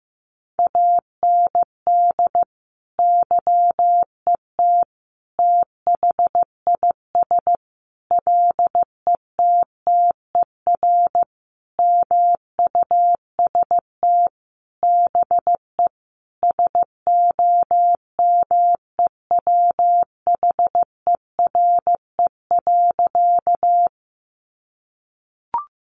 Morse code, heard as a keyed tone, about 15 wpm, 700 Hz. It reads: AND YET THIS LETTER MUST BE SOMEWHERE.
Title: Sentences from Top 1400 Words 15wpm